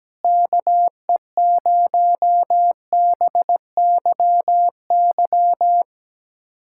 Morse code, heard as a keyed tone, about 17 wpm, 700 Hz. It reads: KE0BYY
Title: USA Callsign DE Callsign - 6 Characters - 17wpm